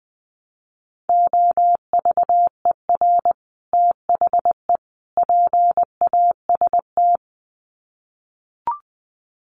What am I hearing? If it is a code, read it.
Morse code, heard as a keyed tone, about 20 wpm, 700 Hz. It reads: OVER THE PAST
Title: Sets of 3 Words 20wpm